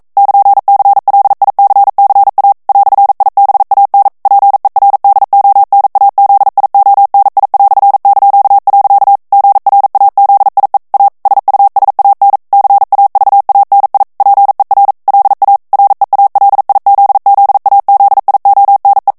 Morse code, heard as a keyed tone, about 42 wpm, 800 Hz. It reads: <KN>CLICCA "IBAN PERDONAZIONI"). GRAZIE A SUSAN CAVANI PER LA REALIZZAZIONE
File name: ITALIEN texte clair 1 vitesse 40 mots minute